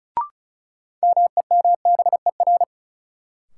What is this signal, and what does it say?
Morse code, heard as a keyed tone, about 36 wpm, 700 Hz. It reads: MEMBER
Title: Sets of 3 Words - New Word Drill - 35wpm